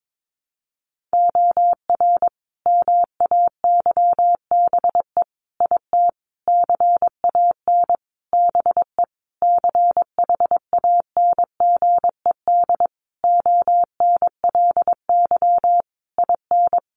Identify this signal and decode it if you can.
Morse code, heard as a keyed tone, about 22 wpm, 700 Hz. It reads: OR MAYBE IT CAN BE CHANGED ONLY IN
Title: A Sound Of Thunder 22wpm